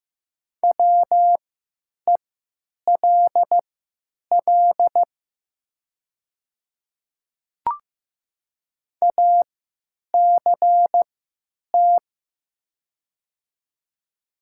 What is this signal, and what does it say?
Morse code, heard as a keyed tone, about 15 wpm, spaced as for 9 wpm, 700 Hz. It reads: WELL ACT
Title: 3x Character Spacing - Top 200 Words - 15wpm